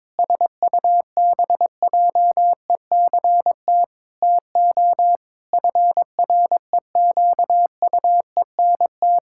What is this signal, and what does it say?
Morse code, heard as a keyed tone, about 22 wpm, 700 Hz. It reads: SUBJECT TO FREQUENT
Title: Pangram Challenge v2 - 22wpm